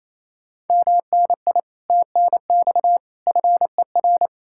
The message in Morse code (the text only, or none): MNI TNX FER